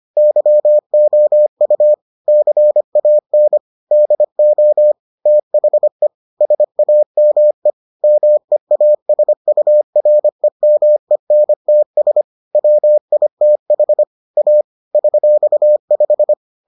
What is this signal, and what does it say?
Morse code, heard as a keyed tone, about 25 wpm, 600 Hz. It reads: YOU CAN DO THE SAME MEASUREMENTS WITH A $5